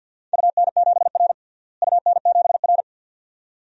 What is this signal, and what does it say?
Morse code, heard as a keyed tone, about 52 wpm, 700 Hz. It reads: UN7R UN7R
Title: DXCC-100 Call Signs - 4 Characters - Repeated 2x - 50wpm